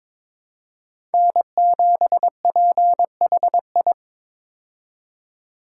Morse code, heard as a keyed tone, about 22 wpm, 700 Hz. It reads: N7PHI
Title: K1USN SST QSO Practice - 22wpm